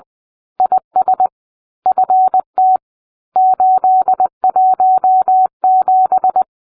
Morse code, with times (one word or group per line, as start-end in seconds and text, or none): 0.6-1.3 IS
1.9-2.8 FT
3.4-6.4 817